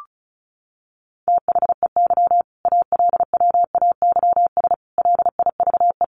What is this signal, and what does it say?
Morse code, heard as a keyed tone, about 34 wpm, 700 Hz. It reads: THEY ALWAYS LIVE
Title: Sentences from Top 1400 Words 35wpm